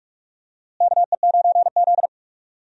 Morse code, heard as a keyed tone, about 46 wpm, 700 Hz. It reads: KE9Z
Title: USA Call Signs - 4 Characters - Repeated 2x - 45wpm